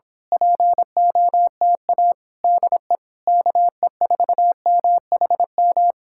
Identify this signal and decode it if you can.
Morse code, heard as a keyed tone, about 26 wpm, 700 Hz. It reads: POTA DE KE4MHM
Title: Speed-Racing Common QSO Phrases Encore - 22wpm